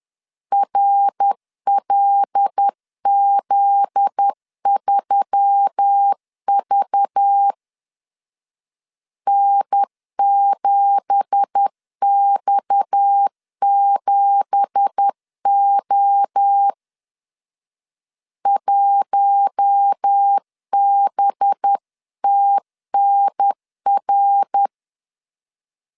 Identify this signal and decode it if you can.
Morse code, heard as a keyed tone, about 10 wpm, 800 Hz. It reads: RLZ3V N7X7O 1BTNR
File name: CW / DET / 1 9 1500C S B 10wpm 4439